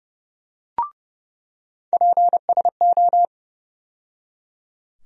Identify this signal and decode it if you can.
Morse code, heard as a keyed tone, about 30 wpm, 700 Hz. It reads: PSO
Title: USA Call Sign Postfixes - 3 Letters 30wpm